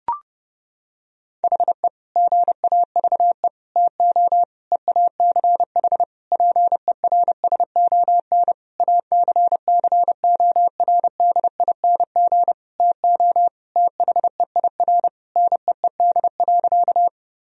Morse code, heard as a keyed tone, about 30 wpm, 700 Hz. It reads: HE GAVE TO EACH PERSON ACCORDING TO THEIR NEED.
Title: Sentences from Top 1000 Words - Review Part 4 - 30wpm